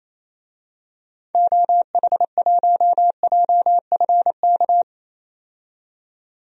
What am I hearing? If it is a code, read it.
Morse code, heard as a keyed tone, about 28 wpm, 700 Hz. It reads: OH1JFK
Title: DXCC-100 Call Signs - 6 Characters - 28wpm